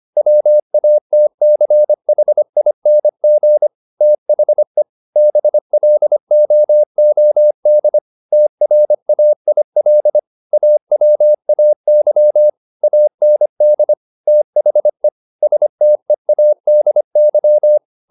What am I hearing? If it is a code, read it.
Morse code, heard as a keyed tone, about 25 wpm, 600 Hz. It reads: WATCHING THE BLOOD TRAIL AWAY AND THE STEADY
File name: the old man and the sea part 1 25wpm